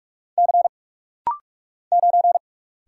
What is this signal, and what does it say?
Morse code, heard as a keyed tone, about 44 wpm, 700 Hz. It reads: C 9
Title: Single Letter-Number - Rapid-Fire 45wpm